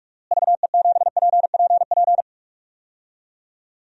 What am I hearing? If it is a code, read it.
Morse code, heard as a keyed tone, about 46 wpm, 700 Hz. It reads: VE7PPP
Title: DXCC-100 Call Signs - 6 Characters - Repeated 3x - 45wpm